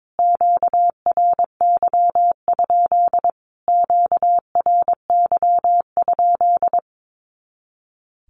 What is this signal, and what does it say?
Morse code, heard as a keyed tone, about 22 wpm, 700 Hz. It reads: QRY? QRY?
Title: Q-Codes Repeated 3x 22wpm